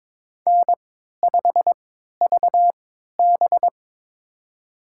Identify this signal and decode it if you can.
Morse code, heard as a keyed tone, about 22 wpm, spaced as for 13 wpm, 700 Hz. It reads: N5VB
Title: USA Call Signs - 4 Characters - 3x Character Spacing - 22wpm